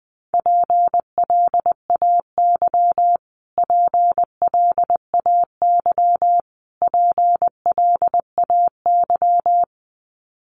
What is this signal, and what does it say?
Morse code, heard as a keyed tone, about 20 wpm, 700 Hz. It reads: PLAY PLAY PLAY
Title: Top 200 Words Repeated 20wpm